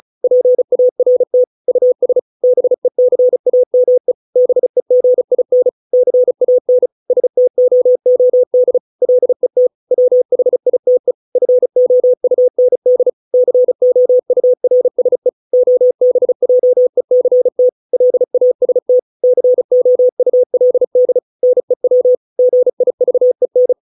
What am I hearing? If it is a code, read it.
Morse code, heard as a keyed tone, about 34 wpm, 500 Hz. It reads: PART US BECAME BEGIN CAN STOOD LET WHITE FOUND COURSE OBJECT LAST COULD NEW GIVEN